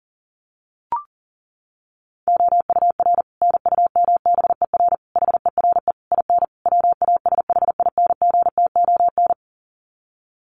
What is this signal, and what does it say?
Morse code, heard as a keyed tone, about 40 wpm, 700 Hz. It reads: OUR NUMBER HERE IN WASHINGTON
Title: Sets of 5 Words Encore2 40wpm